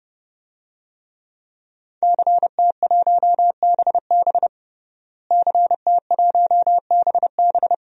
Morse code, heard as a keyed tone, about 32 wpm, 700 Hz. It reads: CT1BB CT1BB
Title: DXCC-100 Call Signs - 5 Characters - Repeated 2x - 30wpm